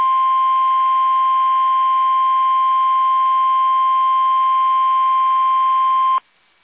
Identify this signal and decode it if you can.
Emergency Alert System (SAME) tones (the 1050 Hz attention tone). It is heard from 0.5 to 5.8 s.